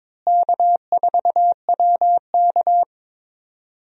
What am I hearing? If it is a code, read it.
Morse code, heard as a keyed tone, about 22 wpm, 700 Hz. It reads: K4WK
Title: USA Call Signs - 4 Characters - Repeated 2x - 22wpm